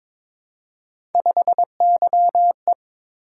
Morse code, heard as a keyed tone, about 22 wpm, 700 Hz. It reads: 5YE
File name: practice / 22wpm / Three Letter-Number Combinations - 22wpm